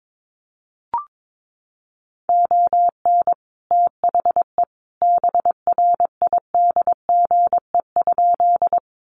Morse code, heard as a keyed tone, about 22 wpm, 700 Hz. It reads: ON THE BRIDGE?
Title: Sentences from Top 1200 Words 22wpm